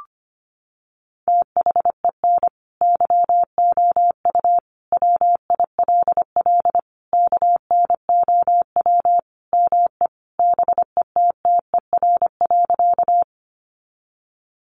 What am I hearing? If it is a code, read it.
Morse code, heard as a keyed tone, about 25 wpm, 700 Hz. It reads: THEN YOU WILL KNOW ME BETTER.
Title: Sentences from Top 400 Words 25wpm